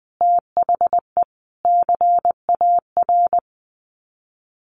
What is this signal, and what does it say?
Morse code, heard as a keyed tone, about 20 wpm, 700 Hz. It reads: THE CAR
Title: Sets of 5 Words Encore 20wpm